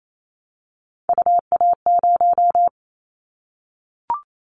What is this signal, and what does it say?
Morse code, heard as a keyed tone, about 28 wpm, 700 Hz. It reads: UA0
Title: Callsign Prefixes for Top 25 DXCC - 28wpm